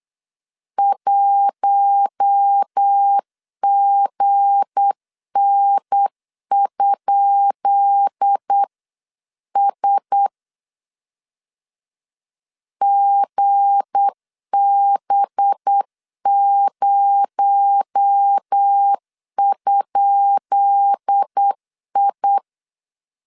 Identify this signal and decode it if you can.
Morse code, heard as a keyed tone, about 8 wpm, 800 Hz. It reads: 1GN?S GB0?I
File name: LEZIONI / 10 SIGNS / 10B INT 8wpm 3534